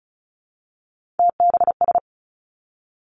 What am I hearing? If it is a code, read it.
Morse code, heard as a keyed tone, about 36 wpm, 700 Hz. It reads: TBS
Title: Three Challenging Letter-Number Combinations 35wpm